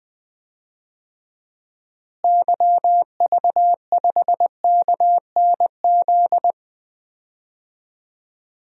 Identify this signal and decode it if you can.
Morse code, heard as a keyed tone, about 20 wpm, 700 Hz. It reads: YV5KNZ
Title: DXCC-100 Call Signs - 6 Characters - Repeated 3x - 20wpm